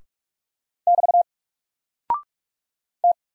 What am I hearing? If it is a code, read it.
Morse code, heard as a keyed tone, about 44 wpm, 700 Hz. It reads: <BT> T
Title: Single Character (Letter:Number:Punctuation:Prosign) Alt 45wpm